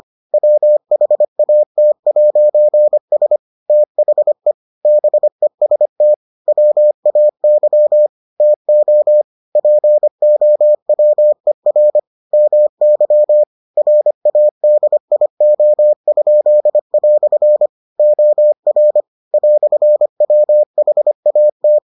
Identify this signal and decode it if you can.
Morse code, heard as a keyed tone, about 25 wpm, 600 Hz. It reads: WHAT'S THE BEST WAY TO POWER MY RADIO?" OR "WHAT